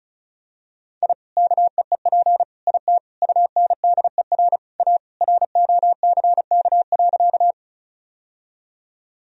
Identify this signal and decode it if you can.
Morse code, heard as a keyed tone, about 34 wpm, 700 Hz. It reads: I KEEP IT UNDER A ROCK.
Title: Sentences from Top 1000 Words - Review Part 2 - 35wpm